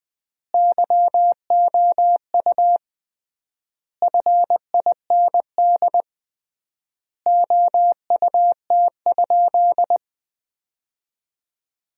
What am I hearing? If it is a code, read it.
Morse code, heard as a keyed tone, about 20 wpm, 700 Hz. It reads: YOU FIND OUT?
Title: 3x Word-Spacing - Sentences from Top 200 Words - 20wpm